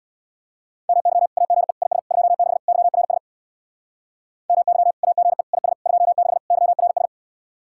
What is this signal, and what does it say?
Morse code, heard as a keyed tone, about 64 wpm, 700 Hz. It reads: MY AGE IS 26 YRS MY AGE IS 26 YRS